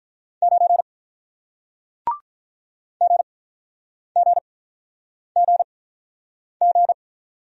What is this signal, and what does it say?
Morse code, heard as a keyed tone, about 48 wpm, spaced as for 9 wpm, 700 Hz. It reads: 9 GGGG